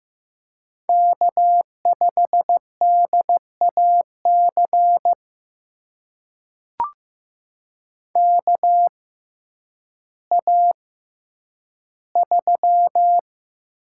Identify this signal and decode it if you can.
Morse code, heard as a keyed tone, about 15 wpm, spaced as for 9 wpm, 700 Hz. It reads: K5DAC KA3
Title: USA Call Signs - 6x Character Spacing - 15wpm